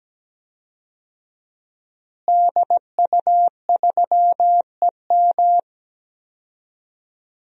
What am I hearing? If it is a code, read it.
Morse code, heard as a keyed tone, about 17 wpm, 700 Hz. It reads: DU3EM
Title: DXCC-100 Call Signs - 5 Characters - Repeated 3x - 17wpm